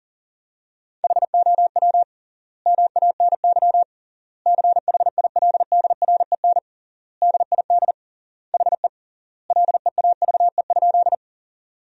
Morse code, heard as a keyed tone, about 40 wpm, 700 Hz. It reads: HOW MANY CHILDREN DID HE LEAVE?
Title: 3x Word-Spacing - Sentences from Top 400 Words - 40wpm